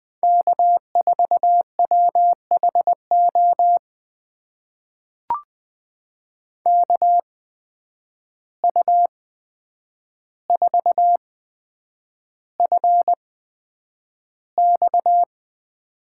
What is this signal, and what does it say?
Morse code, heard as a keyed tone, about 20 wpm, spaced as for 9 wpm, 700 Hz. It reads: K4WHO KU4FX
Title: USA Call Signs - 5 Characters - 8x Character Spacing - 20wpm